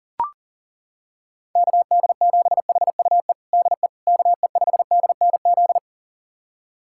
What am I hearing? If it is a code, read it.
Morse code, heard as a keyed tone, about 40 wpm, 700 Hz. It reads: KD7HUE DE KE5DNZ
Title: USA Callsign DE Callsign - 6 Characters - 40wpm